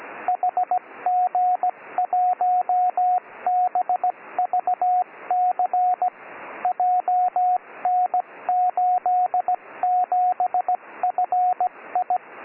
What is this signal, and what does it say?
Morse code, heard as a keyed tone, about 17 wpm, 730 Hz, with radio noise. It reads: HG1BVC JN87FI